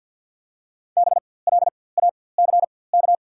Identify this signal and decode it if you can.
Morse code, heard as a keyed tone, about 52 wpm, spaced as for 23 wpm, 700 Hz. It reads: BLACK
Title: 4x Character Spacing - Top 500 Words - 50wpm